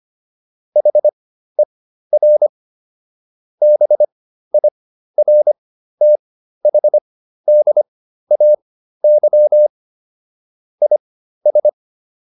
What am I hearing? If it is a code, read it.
Morse code, heard as a keyed tone, about 25 wpm, spaced as for 13 wpm, 600 Hz. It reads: HER BIRTHDAY IS